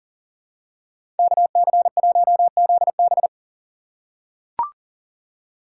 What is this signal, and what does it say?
Morse code, heard as a keyed tone, about 40 wpm, 700 Hz. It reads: KC1ZB
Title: USA Call Signs - 5 Characters - 4x Character Spacing - 40wpm